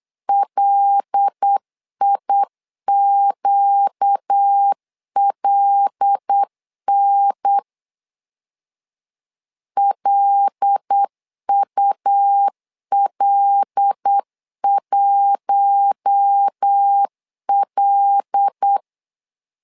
Morse code, heard as a keyed tone, about 8 wpm, 800 Hz. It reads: LIQLN LUL1L
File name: LEZIONI / 5 LBVC / 5 L 8wpm 2831